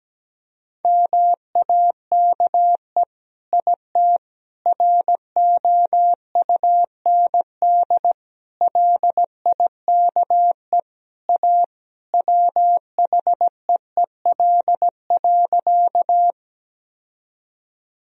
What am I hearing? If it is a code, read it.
Morse code, heard as a keyed tone, about 17 wpm, 700 Hz. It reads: MAKE IT ROUND LIKE A WHEEL.
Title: Sentences from Top 1000 Words - Review Part 3 - 17wpm